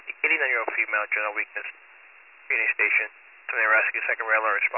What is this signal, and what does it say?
Emergency Alert System (SAME) tones, heard as harsh buzzing bursts, with speech. It is heard from 0.8 to 4.4 s.